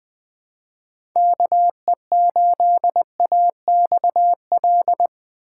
Morse code, heard as a keyed tone, about 20 wpm, 700 Hz. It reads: KE8AXL
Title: USA Call Signs - 6 Characters - 4x Character Spacing - 20wpm